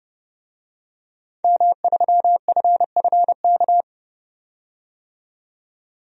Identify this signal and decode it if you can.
Morse code, heard as a keyed tone, about 30 wpm, 700 Hz. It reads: M3FFK
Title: DXCC-100 Call Signs - 5 Characters - 30wpm